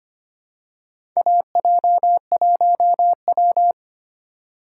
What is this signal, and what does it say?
Morse code, heard as a keyed tone, about 25 wpm, 700 Hz. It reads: AJ1W
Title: USA Call Signs - 4 Characters - 4x Character Spacing - 25wpm